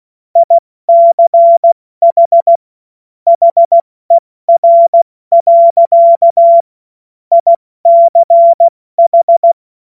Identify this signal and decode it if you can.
Morse code, heard as a keyed tone, about 16 wpm, 680 Hz. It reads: ICH HER. ICH
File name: Morsekurs / Witze / Witz 14. März 2023 70 F80